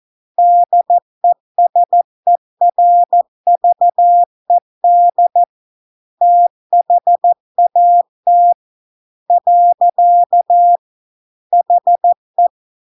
Morse code, heard as a keyed tone, about 14 wpm, 700 Hz. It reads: DESERVED THAT . HE